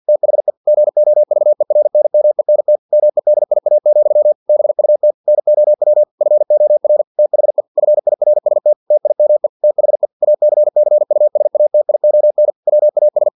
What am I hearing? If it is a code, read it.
Morse code, heard as a keyed tone, about 50 wpm, 600 Hz. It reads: THE GOVERNMENT MEDIA, BUT NOW FOR THE FIRST TIME THE ACCUSATION WAS
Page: Never 49wpm